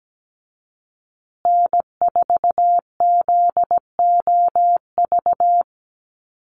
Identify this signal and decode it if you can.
Morse code, heard as a keyed tone, about 17 wpm, 700 Hz. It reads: N4ZOV
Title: USA Call Signs Repeated 3x 17wpm